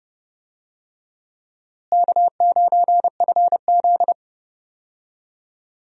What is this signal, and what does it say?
Morse code, heard as a keyed tone, about 30 wpm, 700 Hz. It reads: K9FZ
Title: USA Call Signs - 4 Characters - Repeated 2x - 30wpm